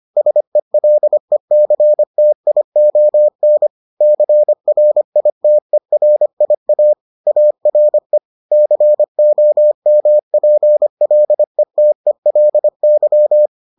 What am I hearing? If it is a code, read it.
Morse code, heard as a keyed tone, about 25 wpm, 600 Hz. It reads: SELECTION CRITERIA ARE COMPLETELY